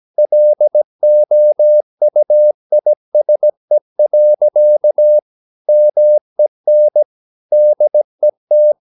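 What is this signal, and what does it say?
Morse code, heard as a keyed tone, about 17 wpm, 600 Hz. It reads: LOUISE. MEN DET
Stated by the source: Slow (QRS) Morse practice at 17 wpm, from CW med Gnister.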